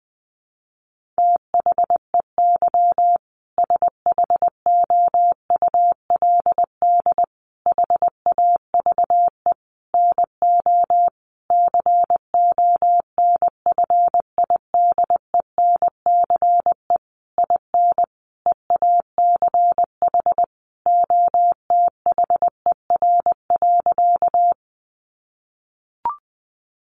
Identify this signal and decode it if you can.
Morse code, heard as a keyed tone, about 20 wpm, 700 Hz. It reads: THEY SHOULD HAVE NO CONFIDENCE IN EACH OTHER.
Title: Sentences from Top 1500 Words 20wpm